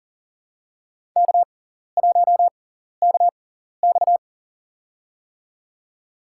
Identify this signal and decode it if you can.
Morse code, heard as a keyed tone, about 40 wpm, spaced as for 14 wpm, 700 Hz. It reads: K1KX
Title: USA Call Signs - 4 Characters - 6x Character Spacing - 40wpm